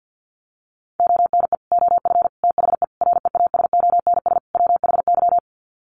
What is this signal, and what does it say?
Morse code, heard as a keyed tone, about 50 wpm, 700 Hz. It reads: ONE OF THE REASONS WHY
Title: Sets of 5 Words 50wpm